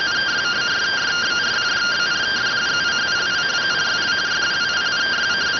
Two-tone teleprinter signal, about 170 Hz shift.